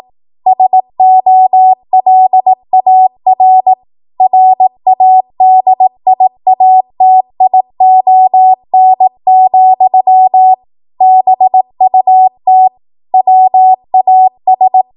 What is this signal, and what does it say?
Morse code, heard as a keyed tone, about 18 wpm, 750 Hz. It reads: SOLAR RADIATION, BUT WAS
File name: Morse / 18WPM